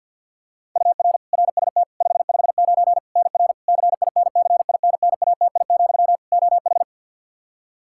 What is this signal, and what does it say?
Morse code, heard as a keyed tone, about 51 wpm, 700 Hz. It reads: UR RST 559 NR CINCINNATI, OH
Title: K1USN SST QSO Practice 50wpm